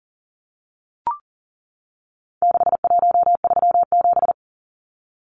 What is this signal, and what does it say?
Morse code, heard as a keyed tone, about 40 wpm, 700 Hz. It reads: 6137
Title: Numbers - 4-Digit 40wpm